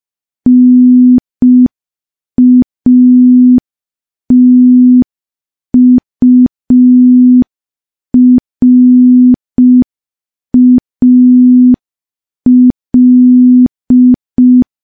Morse code, heard as a keyed tone, about 5 wpm, 250 Hz. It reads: NATURAL
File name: morse / morse-training-05WPM